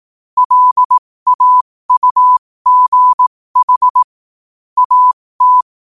Morse code, heard as a keyed tone, about 18 wpm, spaced as for 15 wpm, 1000 Hz. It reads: LAUGH AT
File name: 15wpm